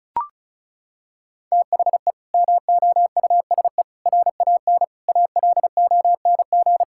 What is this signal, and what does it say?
Morse code, heard as a keyed tone, about 34 wpm, 700 Hz. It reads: THE MOUSE RAN ALONG
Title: Sentences from Top 1000 Words - Review Part 5 - 35wpm